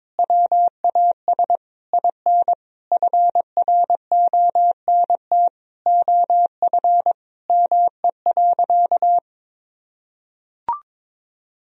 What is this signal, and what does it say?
Morse code, heard as a keyed tone, about 22 wpm, 700 Hz. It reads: WAS IN FRONT OF ME.
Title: Sentences from Top 1000 Words - Review Part 2 - 22wpm